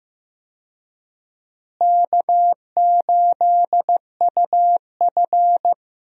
Morse code, heard as a keyed tone, about 15 wpm, 700 Hz. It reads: K8UF
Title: USA Call Signs - 4 Characters - Repeated 2x - 15wpm